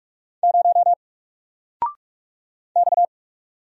Morse code, heard as a keyed tone, about 46 wpm, 700 Hz. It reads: 0 X
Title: Single Letter-Number Alt 45wpm